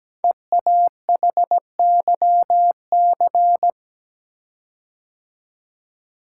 Morse code, heard as a keyed tone, about 17 wpm, 700 Hz. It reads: EAHYC
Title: Code Groups - 5 Letters - 17wpm